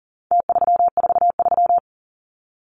Morse code, heard as a keyed tone, about 40 wpm, 700 Hz. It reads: T343
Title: Numbers - Contest Style - 4-Digit 40wpm